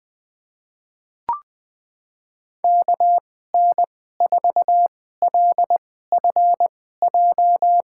Morse code, heard as a keyed tone, about 20 wpm, spaced as for 14 wpm, 700 Hz. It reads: KN4LFJ
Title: USA Call Signs - 2x Character Spacing - 20wpm